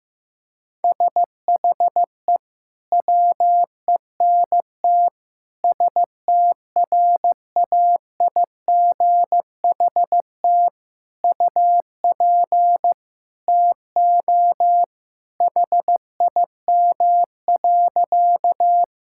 Morse code, heard as a keyed tone, about 15 wpm, 700 Hz. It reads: SHE WENT STRAIGHT UP TO HIM.
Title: Sentences from Top 1000 Words - Review Part 6 - 15wpm